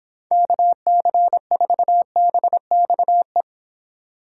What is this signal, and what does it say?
Morse code, heard as a keyed tone, about 26 wpm, 700 Hz. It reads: KC4BXE